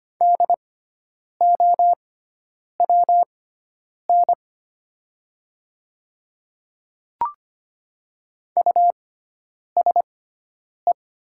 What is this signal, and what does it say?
Morse code, heard as a keyed tone, about 25 wpm, spaced as for 9 wpm, 700 Hz. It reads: DOWN USE
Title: 6x Character Spacing - Top 100 Words - 25wpm